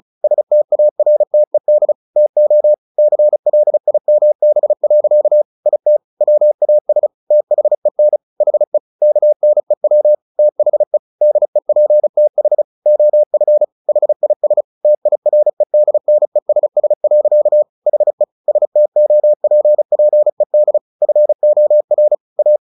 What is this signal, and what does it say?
Morse code, heard as a keyed tone, about 34 wpm, 600 Hz. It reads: STARTED TO CLIMB. IT WAS THEN HE KNEW THE DEPTH OF HIS TIREDNESS. HE STOPPED FOR A
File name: the old man and the sea 2 35wpm